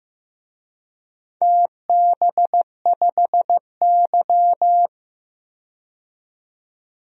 Morse code, heard as a keyed tone, about 15 wpm, 700 Hz. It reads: TB5Y
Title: DXCC-100 Call Signs - 4 Characters - 15wpm